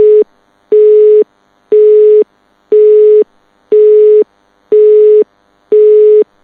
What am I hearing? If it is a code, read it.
Telephone call progress tone (Europe, 425 Hz), heard as a telephone network's tone, about 420 Hz. Busy tone.